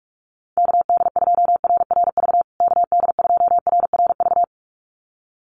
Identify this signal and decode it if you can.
Morse code, heard as a keyed tone, about 44 wpm, 700 Hz. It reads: KD2RRV KD2RRV
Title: USA Call Signs Repeated 2x 45wpm